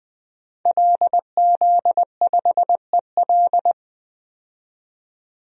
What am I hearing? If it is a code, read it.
Morse code, heard as a keyed tone, about 20 wpm, 700 Hz. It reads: LZ5EL
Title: DXCC-100 Call Signs - 5 Characters - 20wpm